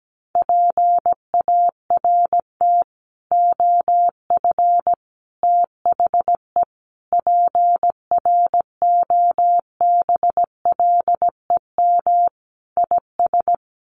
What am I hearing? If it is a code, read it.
Morse code, heard as a keyed tone, about 17 wpm, 700 Hz. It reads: PART OF THE PROBLEM IS
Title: Sets of 5 Words Encore2 17wpm